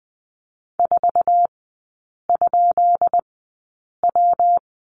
Morse code, heard as a keyed tone, about 20 wpm, spaced as for 8 wpm, 700 Hz. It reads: 4?W
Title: 178 Lesson 29e - Slash - Three Characters 20wpm